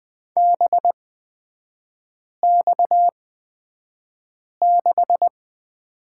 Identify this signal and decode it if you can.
Morse code, heard as a keyed tone, about 20 wpm, spaced as for 5 wpm, 690 Hz. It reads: BX6
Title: 229 Lesson 37e - Letter x - Three Characters 20wpm